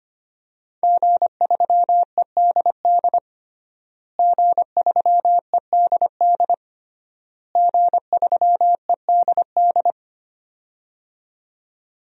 Morse code, heard as a keyed tone, about 25 wpm, 700 Hz. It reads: G3EDD G3EDD G3EDD
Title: DXCC-100 Call Signs - 5 Characters - Repeated 3x - 25wpm